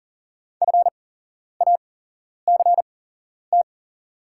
Morse code, heard as a keyed tone, about 41 wpm, spaced as for 11 wpm, 690 Hz. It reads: FACT
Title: 8x Character Spacing - Top 500 Words - 40wpm